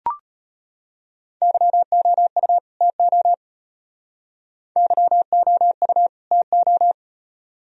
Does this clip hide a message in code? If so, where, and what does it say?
Yes: Morse code from 1.4 to 6.9 s, heard as a keyed tone, about 32 wpm, 700 Hz. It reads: YOU TO YOU TO